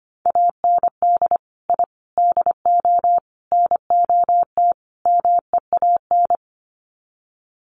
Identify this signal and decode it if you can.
Morse code, heard as a keyed tone, about 25 wpm, 700 Hz. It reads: AND I DO NOT MEAN
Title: Sets of 5 Words Encore2 25wpm